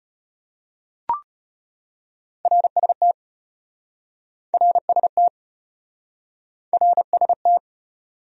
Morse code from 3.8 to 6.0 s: RST

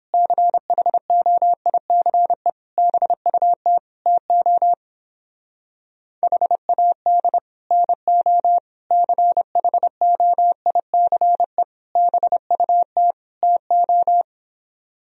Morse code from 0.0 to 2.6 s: CHOICE

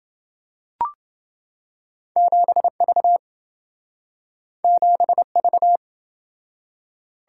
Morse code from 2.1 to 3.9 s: 7V

